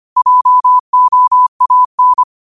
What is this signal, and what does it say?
Morse code, heard as a keyed tone, about 25 wpm, 1000 Hz. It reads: JOAN
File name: qotdfiles / 25wpm